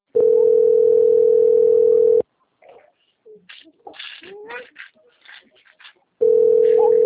Ringback tone (North America), 440 + 480 Hz.